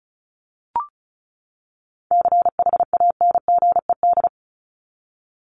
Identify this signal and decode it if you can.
Morse code, heard as a keyed tone, about 34 wpm, 700 Hz. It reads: CHANGED
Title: ED Suffix - Top 100 Words with Base Words 35wpm